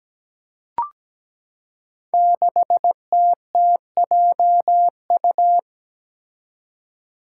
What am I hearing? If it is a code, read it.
Morse code, heard as a keyed tone, about 17 wpm, 700 Hz. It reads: 6TTJU
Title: Five Challenging Letter-Number Combinations 17wpm